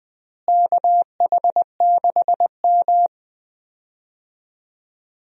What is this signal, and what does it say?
Morse code, heard as a keyed tone, about 20 wpm, 700 Hz. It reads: KH6M
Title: DXCC-100 Call Signs - 4 Characters - Repeated 2x - 20wpm